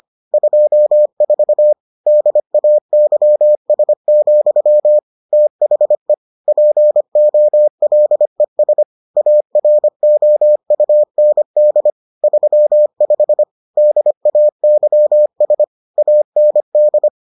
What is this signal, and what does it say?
Morse code, heard as a keyed tone, about 25 wpm, 600 Hz. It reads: 24 DAYS, THE POLES AROUND 35 DAYS AND